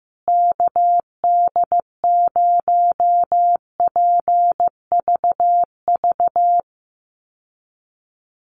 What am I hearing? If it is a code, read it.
Morse code, heard as a keyed tone, about 14 wpm, 700 Hz. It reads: KD0PVV
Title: USA Call Signs Repeated 3x 15wpm